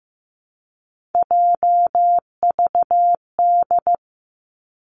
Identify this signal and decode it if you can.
Morse code, heard as a keyed tone, about 15 wpm, 700 Hz. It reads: JVD
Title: Three Challenging Letter-Number Combinations 15wpm